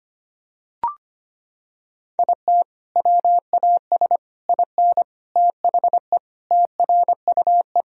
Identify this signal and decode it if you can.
Morse code, heard as a keyed tone, about 25 wpm, 700 Hz. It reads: IT WAS IN THE TRUE
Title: Sentences from Top 1000 Words - Review Part 4 - 25wpm